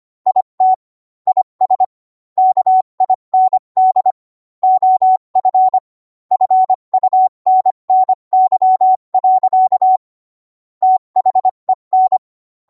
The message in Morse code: IT IS KIND OF FUNNY. THEN